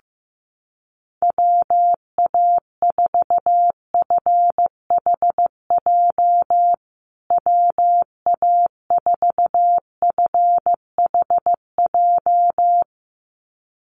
Morse code, heard as a keyed tone, about 15 wpm, 700 Hz. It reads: WA4FHJ WA4FHJ
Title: USA Call Signs Repeated 2x 15wpm